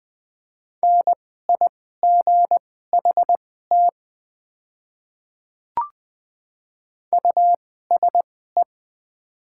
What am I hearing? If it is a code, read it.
Morse code, heard as a keyed tone, about 20 wpm, spaced as for 14 wpm, 700 Hz. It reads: NIGHT USE
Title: 2x Character Spacing - Top 300 Words - 20wpm